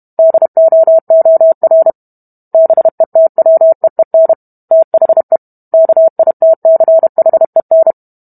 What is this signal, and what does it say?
Morse code, heard as a keyed tone, about 32 wpm, 650 Hz. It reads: DOOR BETWEEN THE KITCHEN